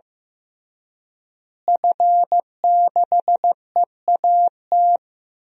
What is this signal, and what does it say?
Morse code, heard as a keyed tone, about 15 wpm, 700 Hz. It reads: F6EAT
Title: DXCC-100 Call Signs - 5 Characters - Repeated 3x - 15wpm